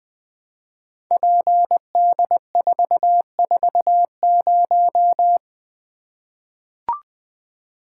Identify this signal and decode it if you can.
Morse code, heard as a keyed tone, about 20 wpm, 700 Hz. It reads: PD440
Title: Code Groups - 5 Letter:Numbers - 20wpm